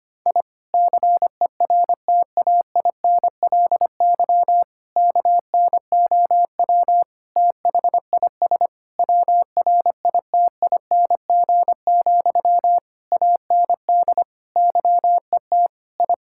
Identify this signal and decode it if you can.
Morse code, heard as a keyed tone, about 25 wpm, 700 Hz. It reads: I CERTAINLY KNOW THIS WRITING, AND YET I
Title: Sentences from Top 1000 Words - Review Part 6 - 25wpm